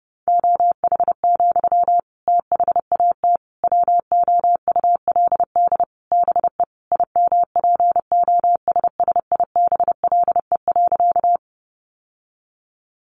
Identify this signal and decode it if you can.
Morse code, heard as a keyed tone, about 30 wpm, 700 Hz. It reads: OH, THAT WOULD BE IMPOSSIBLE.
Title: Sentences from Top 1100 Words 30wpm